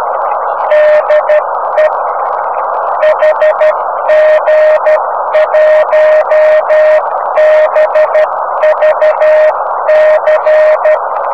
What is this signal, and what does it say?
Morse code, heard as a keyed tone, about 12 wpm, 620 Hz, with noise. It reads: DE HG1BVC
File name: om3pv 70mhz / zaznam / hg1bvc